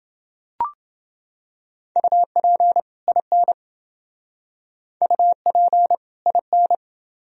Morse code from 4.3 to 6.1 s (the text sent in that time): UP